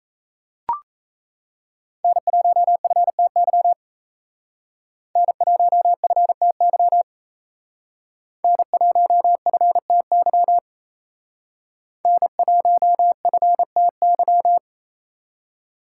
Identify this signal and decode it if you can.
Morse code, heard as a keyed tone, about 32 wpm, 700 Hz. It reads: N1FTY N1FTY N1FTY N1FTY
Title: Speed Racing - USA 1x3 Call Signs 28wpm